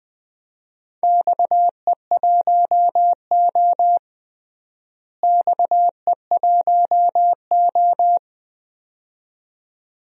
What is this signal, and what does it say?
Morse code, heard as a keyed tone, about 20 wpm, 700 Hz. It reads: XE1O XE1O
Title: DXCC-100 Call Signs - 4 Characters - Repeated 2x - 20wpm